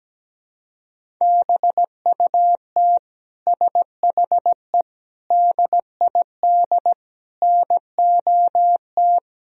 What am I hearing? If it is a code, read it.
Morse code, heard as a keyed tone, about 17 wpm, 700 Hz. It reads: BUT SHE DID NOT
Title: Sentences from Top 1000 Words - Review Part 1 - 17wpm